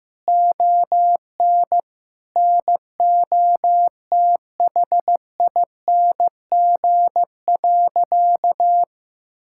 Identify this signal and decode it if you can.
Morse code, heard as a keyed tone, about 16 wpm, 700 Hz. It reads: ON NOTHING.
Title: Sentences from Top 1000 Words - Review Part 5 - 15wpm